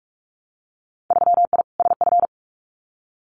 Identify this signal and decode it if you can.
Morse code, heard as a keyed tone, about 44 wpm, 700 Hz. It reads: 3I SF
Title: Winter Field Day Exchanges 45wpm